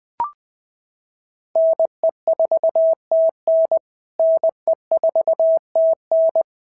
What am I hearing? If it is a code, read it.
Morse code, heard as a keyed tone, about 20 wpm, 660 Hz. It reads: NE4TN NE4TN
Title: POTA QSO Practice - 20wpm